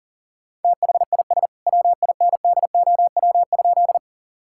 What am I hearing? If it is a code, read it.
Morse code, heard as a keyed tone, about 40 wpm, 700 Hz. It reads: THIS WINDOW?
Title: Sentences from Top 1600 Words 40wpm